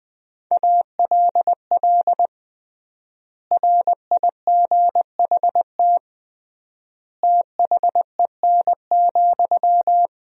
Morse code, heard as a keyed tone, about 20 wpm, 700 Hz. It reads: ALL RIGHT THEN,
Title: 3x Word-Spacing - Sentences from Top 500 Words - 20wpm